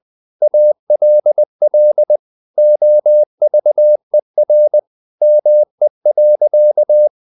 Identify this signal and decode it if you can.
Morse code, heard as a keyed tone, about 20 wpm, 600 Hz. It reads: ALL OVER ME.
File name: the old man and the sea / the old man and the sea part 1 20wpm